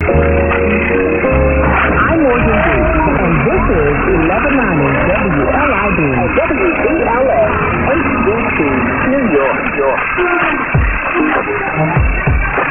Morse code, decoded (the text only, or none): none